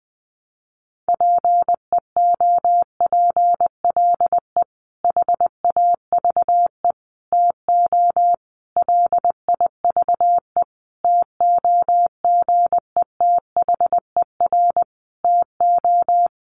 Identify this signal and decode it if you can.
Morse code, heard as a keyed tone, about 20 wpm, 700 Hz. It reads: PEOPLE HAVE TO LIVE TOGETHER TO
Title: Sentences from Top 1100 Words 20wpm